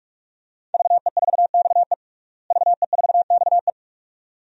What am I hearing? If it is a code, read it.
Morse code, heard as a keyed tone, about 43 wpm, 700 Hz. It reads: VE4XE VE4XE
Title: DXCC-100 Call Signs - 5 Characters - Repeated 2x - 45wpm